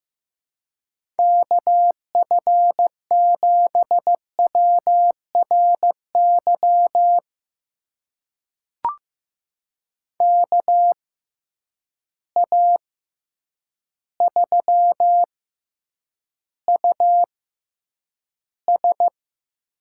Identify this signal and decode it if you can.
Morse code, heard as a keyed tone, about 15 wpm, spaced as for 8 wpm, 700 Hz. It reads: KF7WRY KA3US
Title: USA Call Signs - 6 Characters - 6x Character Spacing - 15wpm